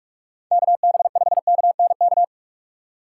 Morse code, heard as a keyed tone, about 41 wpm, 700 Hz. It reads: KB5KNK
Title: USA Call Signs - 6 Characters - 3x Character Spacing - 45wpm